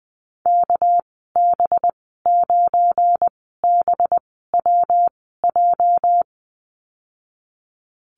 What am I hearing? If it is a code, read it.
Morse code, heard as a keyed tone, about 20 wpm, spaced as for 14 wpm, 700 Hz. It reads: KB9BWJ
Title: 231 Lesson 37g - Letter x - Call Signs 20wpm